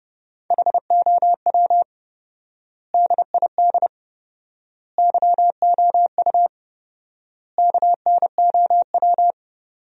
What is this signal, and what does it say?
Morse code, heard as a keyed tone, about 30 wpm, 700 Hz. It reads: HOW DID YOU KNOW
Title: 4x Word-Spacing - Sentences from Top 100 Words - 30wpm